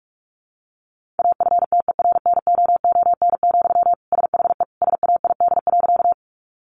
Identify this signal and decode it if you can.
Morse code, heard as a keyed tone, about 44 wpm, 700 Hz. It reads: AFTERNOON, SHE SAID.
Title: Sentences from Top 1500 Words 45wpm